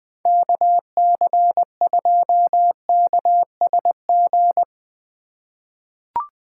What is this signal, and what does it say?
Morse code, heard as a keyed tone, about 20 wpm, 700 Hz. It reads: KC2KSG
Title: USA Call Signs - 6 Characters - 3x Character Spacing - 20wpm